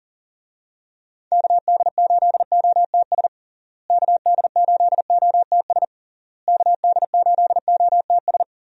Morse code, heard as a keyed tone, about 40 wpm, 700 Hz. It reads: KD8OTS KD8OTS KD8OTS
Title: USA Call Signs - 6 Characters - Repeated 3x - 40wpm